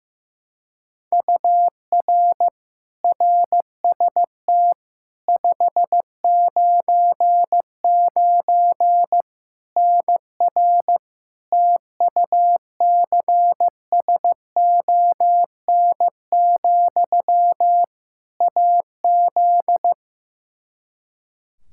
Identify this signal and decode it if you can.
Morse code, heard as a keyed tone, about 15 wpm, 700 Hz. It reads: UR RST 599 NR TUCSON, AZ
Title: Practice QSO - Exchange 1 - Beginning - 15wpm